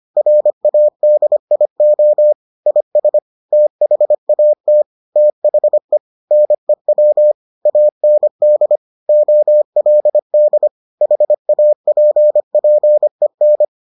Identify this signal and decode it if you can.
Morse code, heard as a keyed tone, about 25 wpm, 600 Hz. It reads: RADIO IS THAT THE NEW AND OLD HAPPEN